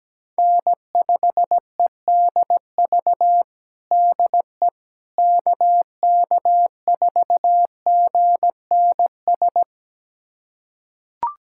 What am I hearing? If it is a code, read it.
Morse code, heard as a keyed tone, about 17 wpm, 700 Hz. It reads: N5EDV DE KK4GNS
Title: USA Callsign DE Callsign 17wpm